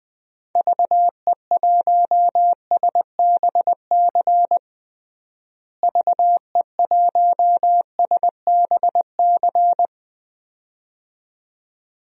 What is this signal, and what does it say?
Morse code, heard as a keyed tone, about 20 wpm, 700 Hz. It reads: VE1SBC VE1SBC
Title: DXCC-100 Call Signs - 6 Characters - Repeated 2x - 20wpm